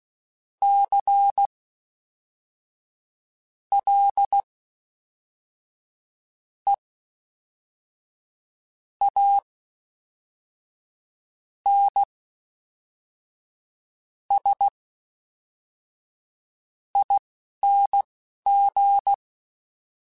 Morse code, heard as a keyed tone, about 16 wpm, spaced as for 4 wpm, 800 Hz. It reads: CLEANSING
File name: Special INGs / Special INGs-mp3s / Fast ing @10